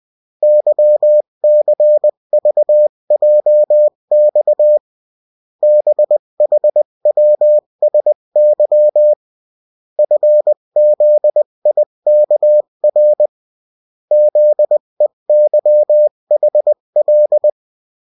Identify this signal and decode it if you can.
Morse code, heard as a keyed tone, about 20 wpm, 600 Hz. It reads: YCVJX BHWSY FZIKR ZEYHL
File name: morse mp3 / revision9-vitesse 18 mots